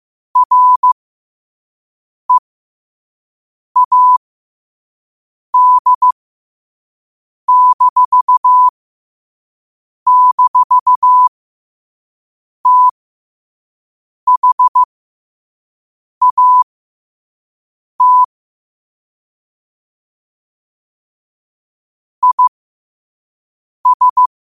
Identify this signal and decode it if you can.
Morse code, heard as a keyed tone, about 15 wpm, spaced as for 5 wpm, 1000 Hz. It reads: READ--THAT IS